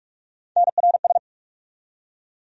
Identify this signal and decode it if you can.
Morse code, heard as a keyed tone, about 47 wpm, 700 Hz. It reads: NRS
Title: USA Call Sign Postfixes - 3 Letters 45wpm